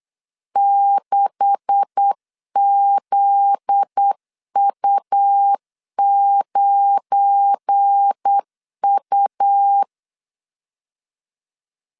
Morse code, heard as a keyed tone, about 8 wpm, 800 Hz. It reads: 6ZU9U
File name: LEZIONI / 10 SIGNS / 10C DOT 8wpm 3649